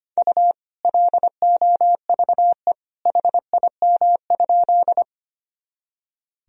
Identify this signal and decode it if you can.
Morse code, heard as a keyed tone, about 25 wpm, 700 Hz. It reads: U LOVE HIM?
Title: Sentences from Top 400 Words and Abbreviations 25wpm